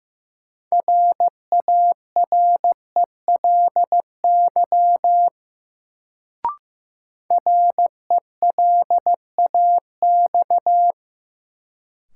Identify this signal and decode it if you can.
Morse code, heard as a keyed tone, about 15 wpm, 700 Hz. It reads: RARELY RELAX
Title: Sentences from Top 1400 Words - New Word Drill Alt - 15wpm